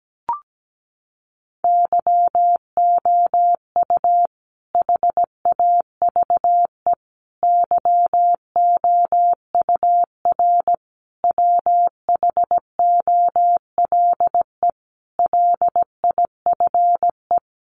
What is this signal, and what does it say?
Morse code, heard as a keyed tone, about 17 wpm, 700 Hz. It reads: YOU HAVE YOUR WHOLE LIFE
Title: Sentences from Top 400 Words 17wpm